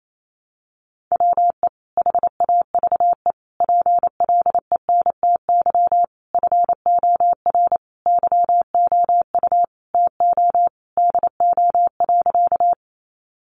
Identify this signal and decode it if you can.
Morse code, heard as a keyed tone, about 28 wpm, 700 Hz. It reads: WE HAVE PLENTY FOR YOU TO DO.
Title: Sentences from Top 1300 Words 28wpm